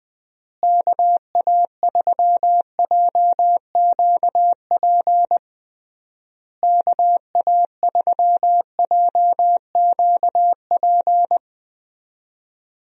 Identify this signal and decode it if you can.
Morse code, heard as a keyed tone, about 20 wpm, 700 Hz. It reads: KA3JQP KA3JQP
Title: USA Call Signs - 6 Characters - Repeated 3x - 20wpm